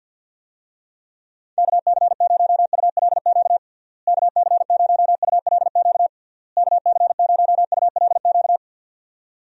Morse code, heard as a keyed tone, about 46 wpm, 700 Hz. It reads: KC0ULQ KC0ULQ KC0ULQ
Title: USA Call Signs - 6 Characters - Repeated 3x - 50wpm